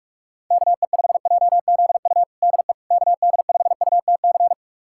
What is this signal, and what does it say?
Morse code, heard as a keyed tone, about 42 wpm, 700 Hz. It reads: KE5JZU DE KD5UTC
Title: USA Callsign DE Callsign 45wpm